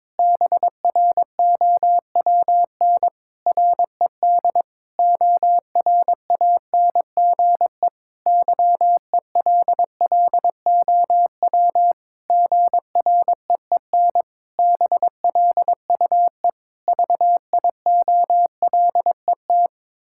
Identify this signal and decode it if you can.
Morse code, heard as a keyed tone, about 21 wpm, 700 Hz. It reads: BROWN RED ORANGE YELLOW GREEN BLUE VIOLET
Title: Pangram Challenge v2 - 22wpm